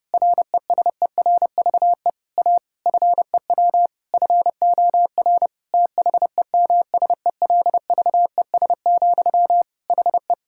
Morse code, heard as a keyed tone, about 30 wpm, 700 Hz. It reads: RESERVE A FEW FOR THEMSELVES, HE
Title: Sentences from Top 1000 Words - Review Part 6 - 30wpm